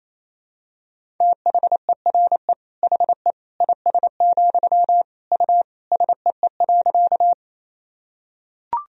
Morse code, heard as a keyed tone, about 28 wpm, 700 Hz. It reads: THERE HE IS, U SEE.
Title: Sentences from Top 100 Words and Abbreviations 28wpm